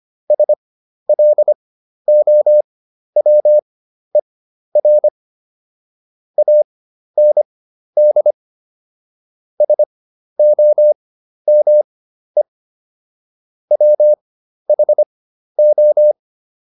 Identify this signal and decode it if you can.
Morse code, heard as a keyed tone, about 25 wpm, spaced as for 12 wpm, 600 Hz. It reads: SLOWER AND SOME WHO